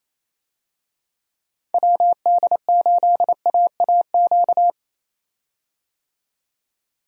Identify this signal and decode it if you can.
Morse code, heard as a keyed tone, about 28 wpm, 700 Hz. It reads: WD8AAQ
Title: USA Call Signs - 6 Characters - 28wpm